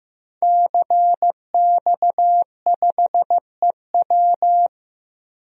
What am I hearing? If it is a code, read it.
Morse code, heard as a keyed tone, about 15 wpm, 700 Hz. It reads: CX5EW
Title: DXCC-100 Call Signs - 5 Characters - 15wpm